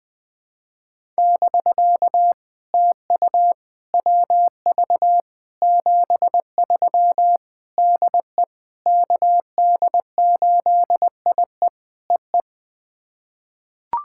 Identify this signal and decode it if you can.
Morse code, heard as a keyed tone, about 21 wpm, 700 Hz. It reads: <BK> TU WV 73 DE KD8IE EE
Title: POTA QSO Practice - 20wpm